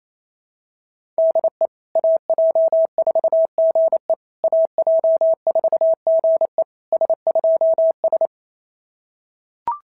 Morse code, heard as a keyed tone, about 28 wpm, 650 Hz. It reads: DE AJ4GE AJ4GE S2S
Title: SOTA QSO Practice 28wpm